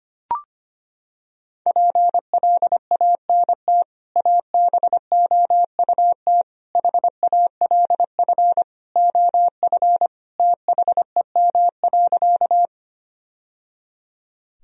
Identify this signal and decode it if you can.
Morse code, heard as a keyed tone, about 25 wpm, 700 Hz. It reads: PLANT ABOUT HALF OF THEM.
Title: Sentences from Top 1000 Words - Review Part 1 - 25wpm